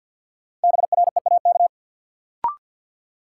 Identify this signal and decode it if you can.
Morse code, heard as a keyed tone, about 49 wpm, 700 Hz. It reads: BREAK
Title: Sentences from Top 900 Words - New Word Drill - 50wpm